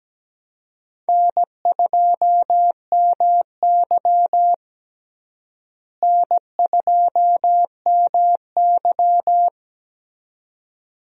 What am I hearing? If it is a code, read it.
Morse code, heard as a keyed tone, about 17 wpm, 700 Hz. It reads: N2MY N2MY
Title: USA Call Signs - 4 Characters - Repeated 2x - 17wpm